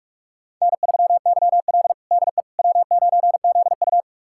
Morse code, heard as a keyed tone, about 41 wpm, 700 Hz. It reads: N3YL DE W9ZU
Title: USA Callsign DE Callsign - 4 Characters - 45wpm